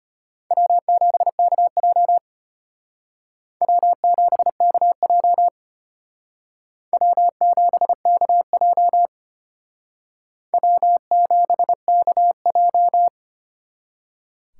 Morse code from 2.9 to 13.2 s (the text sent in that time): W7KJ W7KJ W7KJ